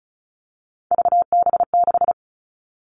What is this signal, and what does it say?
Morse code, heard as a keyed tone, about 34 wpm, 700 Hz. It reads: VB6
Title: All Cut Numbers - 3-Digit 35wpm